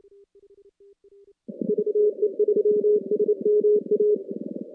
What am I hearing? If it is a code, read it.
Morse code, heard as a keyed tone, about 28 wpm, 420 Hz. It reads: VE3SMA
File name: contests / CW-SS-k5tr- -2200